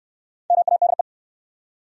Morse code, heard as a keyed tone, about 66 wpm, 700 Hz. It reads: MADE